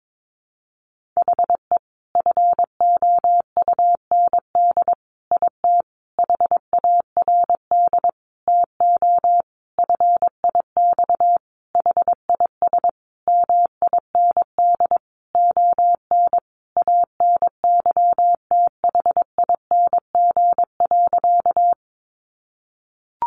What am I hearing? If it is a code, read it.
Morse code, heard as a keyed tone, about 22 wpm, 700 Hz. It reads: HE FOUND IT HARD TO FIX HIS MIND ON ANYTHING.
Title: Sentences from Top 1400 Words 22wpm